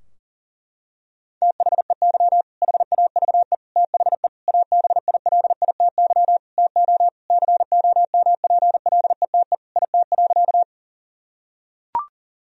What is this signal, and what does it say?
Morse code, heard as a keyed tone, about 37 wpm, 700 Hz. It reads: THEY HAVE THE ABILITY TO COMPLETE IT.
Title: Sentences from Top 1000 Words - Review Part 3 - 40wpm